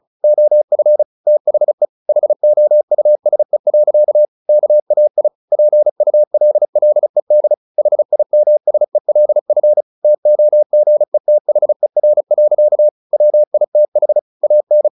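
Morse code, heard as a keyed tone, about 34 wpm, 600 Hz. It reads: OF THE HOUSE. KAI PULLED HIMSELF TOGETHER. WITH AN
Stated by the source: Morse practice at 35 wpm from CW med Gnister, sending Ken Follett's novel Never in English.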